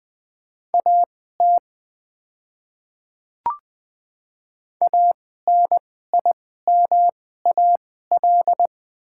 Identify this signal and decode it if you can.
Morse code, heard as a keyed tone, about 20 wpm, spaced as for 14 wpm, 700 Hz. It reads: AT ANIMAL
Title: 2x Character Spacing - Top 200 Words - 20wpm